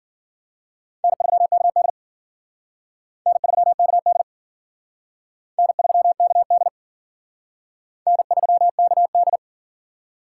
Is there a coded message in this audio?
Yes: Morse code from 1.0 to 9.4 s, heard as a keyed tone, about 51 wpm, 700 Hz. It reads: N3KD N3KD N3KD N3KD